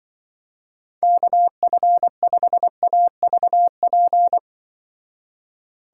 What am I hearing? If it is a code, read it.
Morse code, heard as a keyed tone, about 24 wpm, 700 Hz. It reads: KF5AVP
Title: Speed Racing - USA 2x3 Call Signs 20wpm